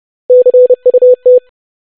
Morse code, heard as a keyed tone, about 30 wpm, 500 Hz. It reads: CUT
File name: basic500-25wpm